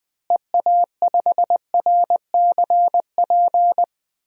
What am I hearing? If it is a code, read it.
Morse code, heard as a keyed tone, about 20 wpm, 700 Hz. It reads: EA5RCP
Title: DXCC-100 Call Signs - 6 Characters - Repeated 3x - 20wpm